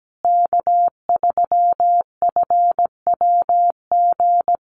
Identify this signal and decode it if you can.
Morse code, heard as a keyed tone, about 17 wpm, 700 Hz. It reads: K3FWG
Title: USA Call Signs Repeated 3x 17wpm